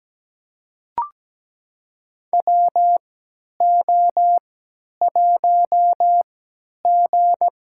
Morse code, heard as a keyed tone, about 17 wpm, spaced as for 10 wpm, 700 Hz. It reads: WO1G
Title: USA Call Signs - 4 Characters - 3x Character Spacing - 17wpm